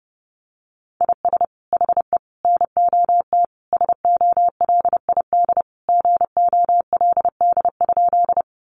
Morse code, heard as a keyed tone, about 28 wpm, 700 Hz. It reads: IS HE NOT SOLID GOLD?
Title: Sentences from Top 1500 Words 30wpm